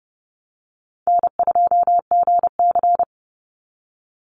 Morse code, heard as a keyed tone, about 32 wpm, 700 Hz. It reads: N2GC
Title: Easy Call Signs - 30wpm